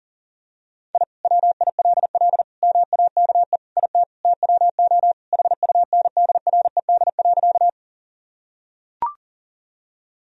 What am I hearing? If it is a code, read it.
Morse code, heard as a keyed tone, about 40 wpm, 700 Hz. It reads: I WILL MAKE IT TWO HUNDRED.
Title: Sentences from Top 1000 Words - Review Part 2 - 40wpm